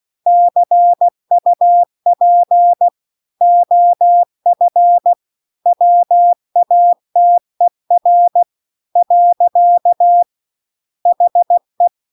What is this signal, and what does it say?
Morse code, heard as a keyed tone, about 16 wpm, 700 Hz. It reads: CUP OF WATER . HE
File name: CW-The-Big-Twoheaded-River-2-16WPM